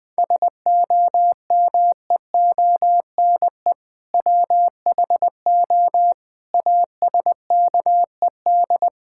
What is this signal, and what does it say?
Morse code, heard as a keyed tone, about 20 wpm, 700 Hz. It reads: SOMEONE WHO ASKED
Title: Sentences with 7-Word Phrases - Spoken Beginning - 20wpm